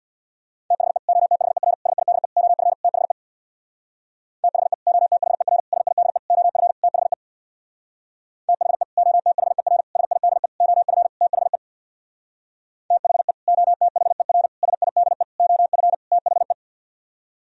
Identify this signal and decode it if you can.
Morse code, heard as a keyed tone, about 64 wpm, 700 Hz. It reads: THE OTHER SIDE OF THE THE OTHER SIDE OF THE THE OTHER SIDE OF THE THE OTHER SIDE OF THE